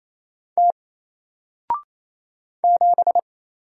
Morse code, heard as a keyed tone, about 28 wpm, 700 Hz. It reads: T 7
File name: Single-Letter-Number-28wpm